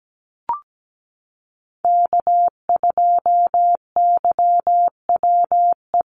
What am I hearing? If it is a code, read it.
Morse code, heard as a keyed tone, about 17 wpm, 700 Hz. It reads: K2YWE
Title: Call Signs Repeated 2x - 17wpm